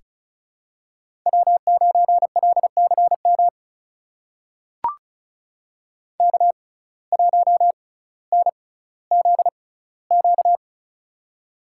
Morse code, heard as a keyed tone, about 36 wpm, spaced as for 18 wpm, 700 Hz. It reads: W9LCM K1NZQ
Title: USA Call Signs - 5 Characters - 6x Character Spacing - 35wpm